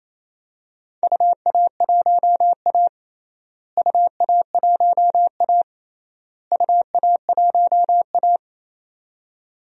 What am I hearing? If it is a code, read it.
Morse code, heard as a keyed tone, about 28 wpm, 700 Hz. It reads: UA1A UA1A UA1A
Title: DXCC-100 Call Signs - 4 Characters - Repeated 3x - 28wpm